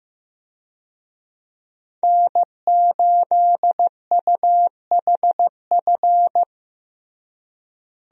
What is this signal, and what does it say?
Morse code, heard as a keyed tone, about 15 wpm, 700 Hz. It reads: N8UHF
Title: USA Call Signs - 5 Characters - Repeated 2x - 15wpm